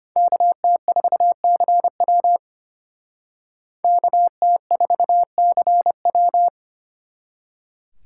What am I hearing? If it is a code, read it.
Morse code, heard as a keyed tone, about 32 wpm, 700 Hz. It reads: KT4CW KT4CW